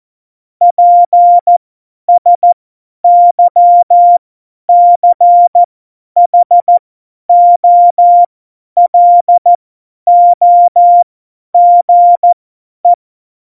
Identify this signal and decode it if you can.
Morse code, heard as a keyed tone, about 14 wpm, spaced as for 10 wpm, 700 Hz. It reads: PSYCHOLOGE
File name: Morsekurs / Witze / Witz 27. Februar 2018, 50 F70